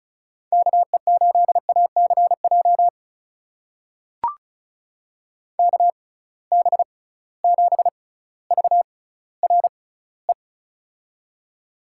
Morse code, heard as a keyed tone, about 36 wpm, spaced as for 18 wpm, 700 Hz. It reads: KE8ACJ KB7VRE
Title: USA Call Signs - 6 Characters - 6x Character Spacing - 35wpm